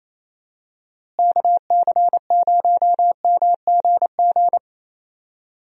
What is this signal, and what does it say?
Morse code, heard as a keyed tone, about 28 wpm, 700 Hz. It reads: KC0MGG
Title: USA Call Signs - 6 Characters - Repeated 2x - 28wpm